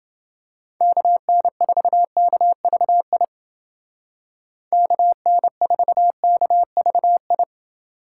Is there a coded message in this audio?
Yes: Morse code from 0.8 to 7.4 s, heard as a keyed tone, about 29 wpm, 700 Hz. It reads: KN4KVI KN4KVI